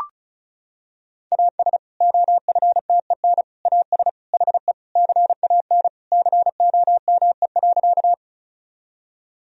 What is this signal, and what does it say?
Morse code, heard as a keyed tone, about 36 wpm, 700 Hz. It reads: AS OFTEN AS HE CAN COME.
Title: Sentences from Top 1000 Words - Review Part 1 - 35wpm